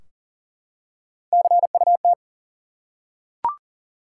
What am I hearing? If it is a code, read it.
Morse code, heard as a keyed tone, about 40 wpm, 700 Hz. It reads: CUT
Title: Three Letter Words - 40wpm